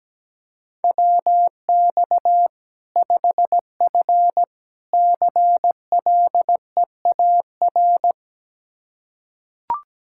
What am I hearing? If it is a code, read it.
Morse code, heard as a keyed tone, about 17 wpm, 700 Hz. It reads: WX 5F CLEAR
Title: Practice QSO - Exchange 2 - Info - 17wpm